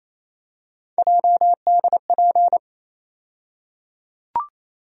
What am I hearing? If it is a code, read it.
Morse code, heard as a keyed tone, about 28 wpm, 700 Hz. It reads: JDP
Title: USA Call Sign Postfixes - 3 Letters 28wpm